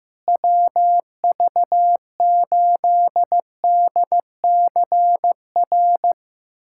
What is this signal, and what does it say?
Morse code, heard as a keyed tone, about 15 wpm, 700 Hz. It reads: WV8DCR
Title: USA Callsign DE Callsign 15wpm